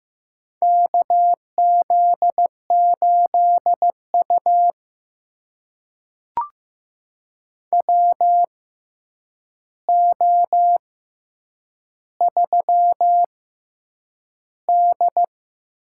Morse code, heard as a keyed tone, about 15 wpm, spaced as for 8 wpm, 700 Hz. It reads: KZ8U WO3D
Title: USA Call Signs - 4 Characters - 6x Character Spacing - 15wpm